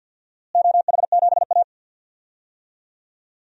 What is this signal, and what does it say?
Morse code, heard as a keyed tone, about 52 wpm, 700 Hz. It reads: OH7A
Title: DXCC-100 Call Signs - 4 Characters - Repeated 2x - 50wpm